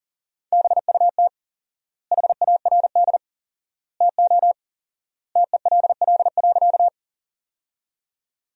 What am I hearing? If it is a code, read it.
Morse code, heard as a keyed tone, about 40 wpm, 700 Hz. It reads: BUT HARD TO TELL.
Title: 4x Word-Spacing - Sentences from Top 300 Words - 40wpm